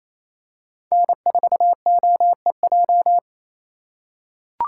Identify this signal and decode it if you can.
Morse code, heard as a keyed tone, about 28 wpm, 700 Hz. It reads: N4OEJ
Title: USA Call Signs - 3x Character Spacing - 28wpm